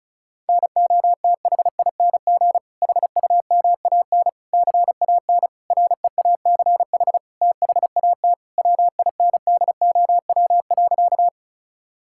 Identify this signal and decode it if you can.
Morse code, heard as a keyed tone, about 36 wpm, 700 Hz. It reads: NOTHING HUMAN CAN REACH THAT WINDOW.
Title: Sentences from Top 1600 Words 35wpm